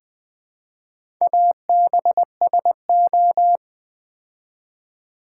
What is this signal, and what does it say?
Morse code, heard as a keyed tone, about 20 wpm, 700 Hz. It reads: ABSO
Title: Build A Word - 2 Letters At A Time - 8-Letter Words - 20wpm